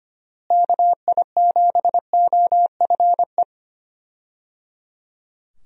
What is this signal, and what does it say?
Morse code, heard as a keyed tone, about 25 wpm, 700 Hz. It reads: KI7OFE
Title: USA Call Signs - 6 Characters - Repeated 3x - 25wpm